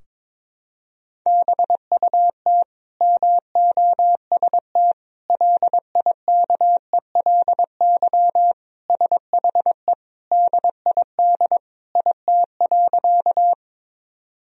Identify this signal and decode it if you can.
Morse code, heard as a keyed tone, about 22 wpm, 700 Hz. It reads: BUT MOST LIKELY SHE DID IT.
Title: Sentences from Top 1000 Words - Review Part 4 - 22wpm